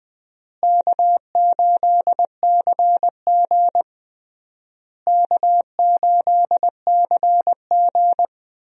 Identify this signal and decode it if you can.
Morse code, heard as a keyed tone, about 20 wpm, 700 Hz. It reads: K8CG K8CG
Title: USA Call Signs - 4 Characters - Repeated 3x - 20wpm